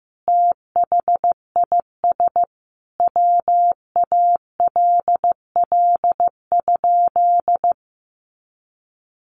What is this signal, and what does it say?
Morse code, heard as a keyed tone, about 15 wpm, 700 Hz. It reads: THIS WALL?
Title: Sentences from Top 1100 Words 15wpm